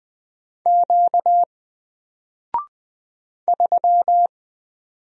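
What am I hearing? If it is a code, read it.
Morse code, heard as a keyed tone, about 20 wpm, 700 Hz. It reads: Q 3
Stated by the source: Single letter and number Morse drill at 20 wpm, from N4LCD.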